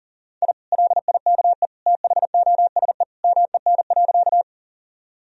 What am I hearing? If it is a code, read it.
Morse code, heard as a keyed tone, about 40 wpm, 700 Hz. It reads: I LIKE THOSE MEN.
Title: Sentences from Top 1000 Words - Review Part 1 - 40wpm